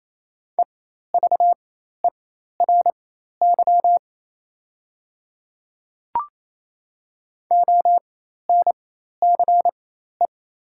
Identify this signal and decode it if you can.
Morse code, heard as a keyed tone, about 27 wpm, spaced as for 13 wpm, 700 Hz. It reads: EVERY ONCE
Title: 4x Character Spacing - Top 300 Words - 28wpm